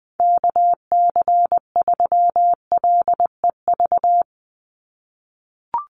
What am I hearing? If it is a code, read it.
Morse code, heard as a keyed tone, about 20 wpm, 700 Hz. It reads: KC3LEV
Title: USA Call Signs 20wpm